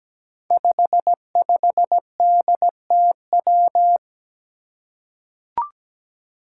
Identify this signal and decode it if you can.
Morse code, heard as a keyed tone, about 17 wpm, 700 Hz. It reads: 55DTW
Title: Five Challenging Letter-Number Combinations 17wpm